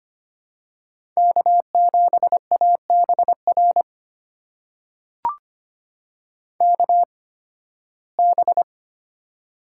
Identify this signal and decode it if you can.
Morse code, heard as a keyed tone, about 25 wpm, spaced as for 16 wpm, 700 Hz. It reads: K7ABR KB
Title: USA Call Signs - 8x Character Spacing - 25wpm